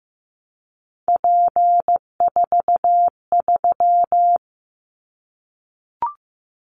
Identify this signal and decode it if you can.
Morse code, heard as a keyed tone, about 15 wpm, 700 Hz. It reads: P43
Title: Callsign Prefixes for Top 100 DXCC - 15wpm